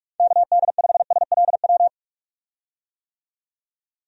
Morse code, heard as a keyed tone, about 46 wpm, 700 Hz. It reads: KD5SLW
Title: USA Call Signs - 6 Characters - 45wpm